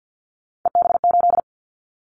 Morse code, heard as a keyed tone, about 50 wpm, 700 Hz. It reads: EB8
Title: Callsign Prefixes for Top 50 DXCC - 50wpm